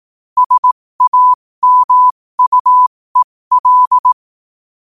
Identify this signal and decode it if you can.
Morse code, heard as a keyed tone, about 18 wpm, spaced as for 15 wpm, 1000 Hz. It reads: SAMUEL
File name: qotdfiles / 15wpm